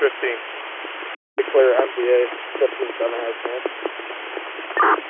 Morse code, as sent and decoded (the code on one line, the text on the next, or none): none